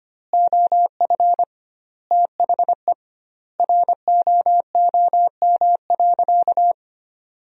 Morse code, heard as a keyed tone, about 25 wpm, 700 Hz. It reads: OF THE ROOM.
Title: 2x Word-Spacing - Sentences from Top 500 Words - 25wpm